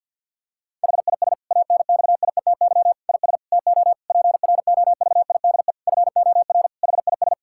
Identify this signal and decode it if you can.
Morse code, heard as a keyed tone, about 50 wpm, 700 Hz. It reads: HIS ANXIETY IS TO PROVIDE FOR HIS
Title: Sentences from Top 1600 Words 50wpm